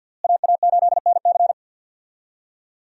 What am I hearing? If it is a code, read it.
Morse code, heard as a keyed tone, about 51 wpm, 700 Hz. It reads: AA8NC
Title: USA Call Signs - 5 Characters - Repeated 3x - 50wpm